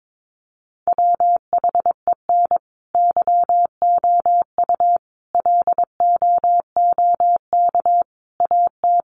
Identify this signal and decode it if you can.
Morse code, heard as a keyed tone, about 22 wpm, 700 Hz. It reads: WHEN YOU LOOK AT
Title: Sets of 5 Words 22wpm